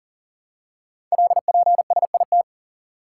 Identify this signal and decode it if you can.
Morse code, heard as a keyed tone, about 42 wpm, 700 Hz. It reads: LPSIT
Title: Code Groups - 5 Letters - 40wpm